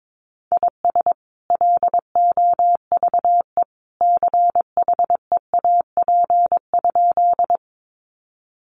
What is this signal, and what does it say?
Morse code, heard as a keyed tone, about 22 wpm, 700 Hz. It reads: IS LOVE CHEAP?
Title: Sentences from Top 1200 Words 22wpm